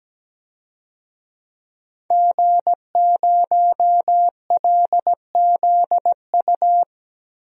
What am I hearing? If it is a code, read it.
Morse code, heard as a keyed tone, about 17 wpm, 700 Hz. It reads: G0LZU
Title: DXCC-100 Call Signs - 5 Characters - Repeated 3x - 17wpm